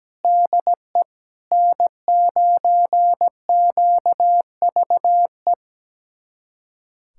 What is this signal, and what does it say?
Morse code, heard as a keyed tone, about 17 wpm, 700 Hz. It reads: DE N9QVE
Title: USA Callsign DE Callsign 17wpm